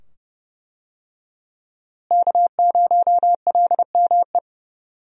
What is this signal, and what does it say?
Morse code, heard as a keyed tone, about 31 wpm, 700 Hz. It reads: K0LME
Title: USA Call Signs - 5 Characters - Repeated 2x - 30wpm